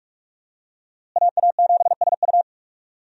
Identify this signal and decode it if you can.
Morse code, heard as a keyed tone, about 47 wpm, 700 Hz. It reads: AA7SU
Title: USA Call Signs - 5 Characters - Repeated 3x - 45wpm